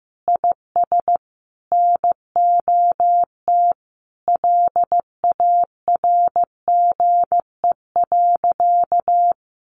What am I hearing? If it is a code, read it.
Morse code, heard as a keyed tone, about 15 wpm, 700 Hz. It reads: IS NOT LARGE.
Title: Sentences from Top 700 Words 15wpm